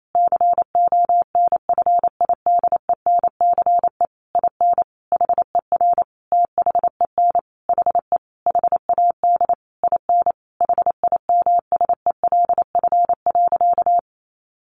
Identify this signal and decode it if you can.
Morse code, heard as a keyed tone, about 28 wpm, 700 Hz. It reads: CONFIDENCE IN HER THEN HE HAD IN HIMSELF.
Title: Sentences from Top 1500 Words 28wpm